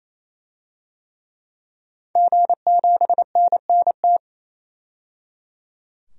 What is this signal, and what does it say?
Morse code, heard as a keyed tone, about 28 wpm, 700 Hz. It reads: G7NNT
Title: DXCC-100 Call Signs - 5 Characters - 28wpm